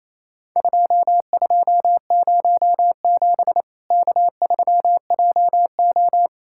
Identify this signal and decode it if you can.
Morse code, heard as a keyed tone, about 27 wpm, 700 Hz. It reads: 2207 K3JO
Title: Worked All Europe DX Contest - QTC Exchange - 28wpm